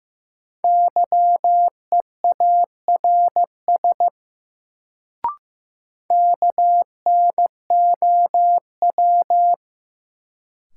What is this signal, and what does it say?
Morse code, heard as a keyed tone, about 15 wpm, 700 Hz. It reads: YEARS KNOW
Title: Sets of 2 Words - New Word Drill - 15wpm